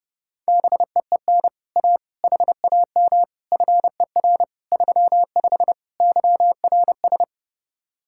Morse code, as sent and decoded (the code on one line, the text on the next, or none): -... . -. / .- / .... .- -- / ..-. .-. / ...-- ..... / -.-- .-. ...
BEEN A HAM FER 35 YRS